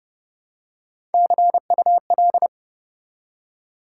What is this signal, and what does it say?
Morse code, heard as a keyed tone, about 28 wpm, 700 Hz. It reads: CUL